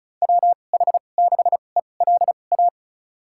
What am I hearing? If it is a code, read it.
Morse code, heard as a keyed tone, about 36 wpm, spaced as for 25 wpm, 700 Hz. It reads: WH6ELA
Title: USA Call Signs - 6 Characters - 2x Character Spacing - 35wpm